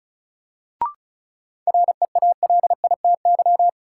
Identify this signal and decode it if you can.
Morse code, heard as a keyed tone, about 36 wpm, 700 Hz. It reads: REALITY
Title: Sentences from Top 1200 Words - New Word Drill Alt - 35wpm